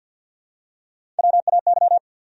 Morse code, heard as a keyed tone, about 49 wpm, 700 Hz. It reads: WAY
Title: Top 100 Words 50wpm